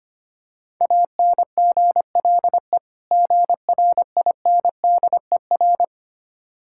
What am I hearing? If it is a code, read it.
Morse code, heard as a keyed tone, about 25 wpm, 690 Hz. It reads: ANGLE GRINDER
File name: practice / 25wpm / Difficult Words with Hints - 25wpm